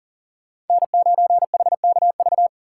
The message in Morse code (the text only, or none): N9HKV